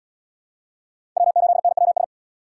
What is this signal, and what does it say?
Morse code, heard as a keyed tone, about 68 wpm, 700 Hz. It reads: W8NWS